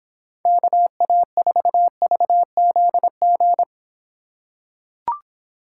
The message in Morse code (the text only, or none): KA4VZG